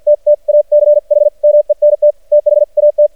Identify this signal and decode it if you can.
Morse code, heard as a keyed tone, about 50 wpm, spaced as for 41 wpm, 590 Hz. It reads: TTACHMENT THAT
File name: archived / CRSnet